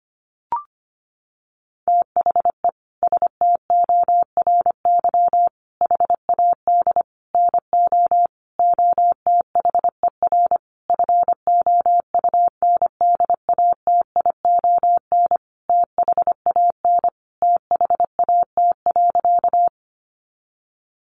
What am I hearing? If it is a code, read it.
Morse code, heard as a keyed tone, about 25 wpm, 700 Hz. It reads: THE STORY HAD NO OTHER FOUNDATION THAN THAT.
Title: Sentences from Top 1200 Words 25wpm